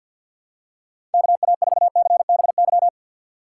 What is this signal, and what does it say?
Morse code, heard as a keyed tone, about 52 wpm, 700 Hz. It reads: KA4CBY
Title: USA Call Signs - 6 Characters - 6x Character Spacing - 50wpm